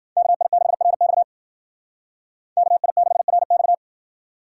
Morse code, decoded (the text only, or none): KI6RX KI6RX